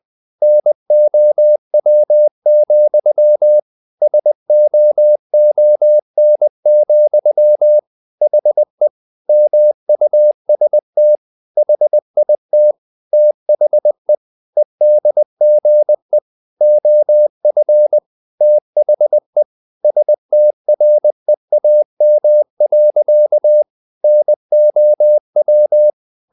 Morse code, heard as a keyed tone, about 20 wpm, 600 Hz. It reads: NOW, SOON, HE MUST HIT THE EDGE OF THE STREAM. NOW
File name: the old man and the sea part 2 20wpm